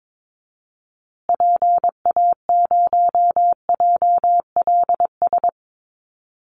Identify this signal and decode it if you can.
Morse code, heard as a keyed tone, about 22 wpm, 700 Hz. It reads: PA0JLS
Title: Call Signs Repeated 3x - 22wpm